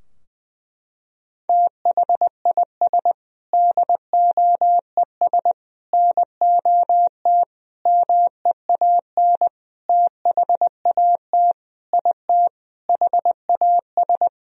Morse code, heard as a keyed tone, about 20 wpm, 700 Hz. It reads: THIS DOES NOT MEAN THAT IT HAS
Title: Sentences from Top 1000 Words - Review Part 4 - 20wpm